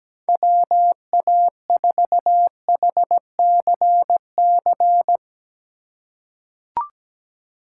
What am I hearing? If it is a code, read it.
Morse code, heard as a keyed tone, about 17 wpm, 700 Hz. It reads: WA4HCC
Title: K1USN SST QSO Practice - 17wpm